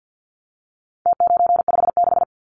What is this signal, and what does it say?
Morse code, heard as a keyed tone, about 50 wpm, 700 Hz. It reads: T956
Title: Numbers - Contest Style - 4-Digit 50wpm